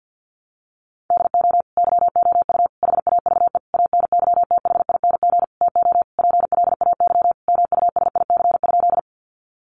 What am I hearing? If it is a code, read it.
Morse code, heard as a keyed tone, about 50 wpm, 700 Hz. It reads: DO YOU HAVE ANYTHING TO PLAY MUSIC?
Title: Sentences from Top 1000 Words 50wpm